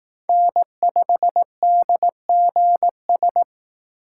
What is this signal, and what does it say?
Morse code, heard as a keyed tone, about 18 wpm, 700 Hz. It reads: N5DGS
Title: Speed Racing - USA 1x3 Call Signs 15wpm